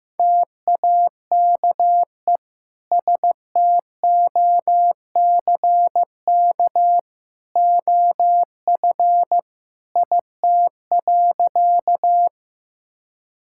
Morse code, heard as a keyed tone, about 16 wpm, 700 Hz. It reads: TAKE STOCK OF IT.
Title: Sentences from Top 1000 Words - Review Part 5 - 15wpm